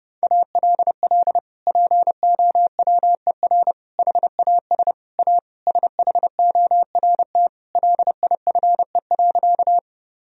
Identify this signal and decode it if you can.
Morse code, heard as a keyed tone, about 30 wpm, 700 Hz. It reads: ALL POWER HAS A SHORT LIFE.
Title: Sentences from Top 1000 Words - Review Part 2 - 30wpm